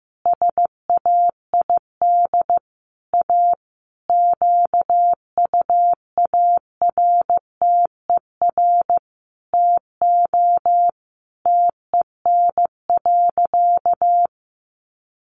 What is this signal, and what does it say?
Morse code, heard as a keyed tone, about 15 wpm, 700 Hz. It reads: SAID A QUARTER TO TEN.
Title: Sentences from Top 1300 Words 15wpm